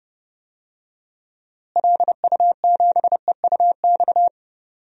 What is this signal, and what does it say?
Morse code, heard as a keyed tone, about 30 wpm, 700 Hz. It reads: LU7EUX
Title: DXCC-100 Call Signs - 6 Characters - Repeated 3x - 30wpm